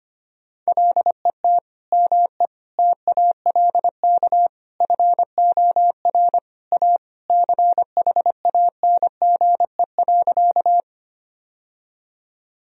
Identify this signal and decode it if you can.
Morse code, heard as a keyed tone, about 25 wpm, 700 Hz. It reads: LET ME TALK FOR A CHANGE.
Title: Sentences from Top 1000 Words - Review Part 1 - 25wpm